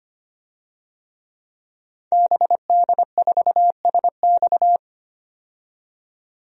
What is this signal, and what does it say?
Morse code, heard as a keyed tone, about 25 wpm, 700 Hz. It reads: BD4SX
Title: DXCC-100 Call Signs - 5 Characters - 25wpm